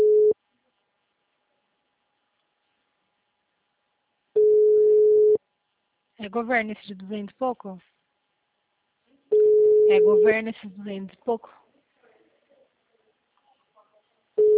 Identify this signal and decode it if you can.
Ringback tone (Europe, 425 Hz), with speech.